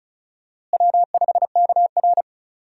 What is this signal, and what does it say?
Morse code, heard as a keyed tone, about 34 wpm, 700 Hz. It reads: W5KR
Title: USA Call Signs - 4 Characters - Repeated 2x - 35wpm